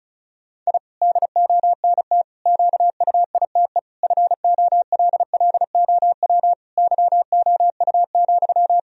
Morse code, heard as a keyed tone, about 36 wpm, 700 Hz. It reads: I DONT QUITE FOLLOW YOU,
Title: Sentences from Top 1000 Words - Review Part 4 - 35wpm